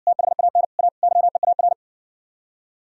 Morse code, heard as a keyed tone, about 59 wpm, 700 Hz. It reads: THAN A YEAR